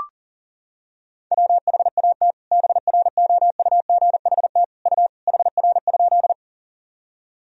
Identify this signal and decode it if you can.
Morse code, heard as a keyed tone, about 40 wpm, 700 Hz. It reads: WHAT BROUGHT U HR?
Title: Sentences from Top 500 Words and Abbreviations 40wpm